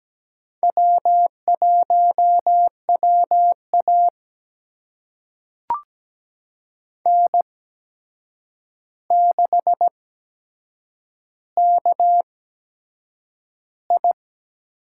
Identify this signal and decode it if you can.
Morse code, heard as a keyed tone, about 17 wpm, spaced as for 7 wpm, 700 Hz. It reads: W1WA N6KI
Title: USA Call Signs - 4 Characters - 8x Character Spacing - 17wpm